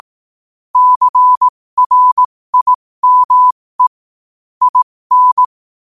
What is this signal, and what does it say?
Morse code, heard as a keyed tone, about 18 wpm, spaced as for 15 wpm, 1000 Hz. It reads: CRIME IN
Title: 15 WPM morse code quotes for Fri, 01 Mar 2024